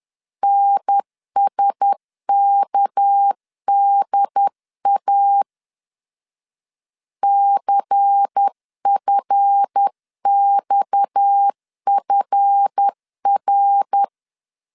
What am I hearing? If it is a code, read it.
Morse code, heard as a keyed tone, about 10 wpm, 800 Hz. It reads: NSKDA CFXFR
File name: LEZIONI / 7 GF9Z / 7 GF 10wpm 2514